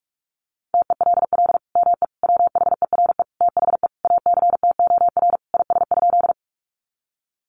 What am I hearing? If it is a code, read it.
Morse code, heard as a keyed tone, about 44 wpm, 700 Hz. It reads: TELL ME WHERE THE ACTOR IS?
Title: Sentences from Top 1500 Words 45wpm